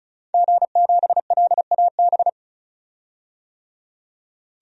Morse code, heard as a keyed tone, about 34 wpm, 700 Hz. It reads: G7LAB
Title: DXCC-100 Call Signs - 5 Characters - 35wpm